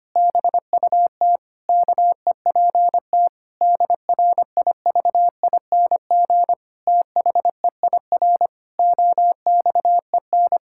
Morse code, heard as a keyed tone, about 25 wpm, 700 Hz. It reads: BUT KEPT DRIVING THEIR OXEN
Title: Pangram Challenge v2 - 25wpm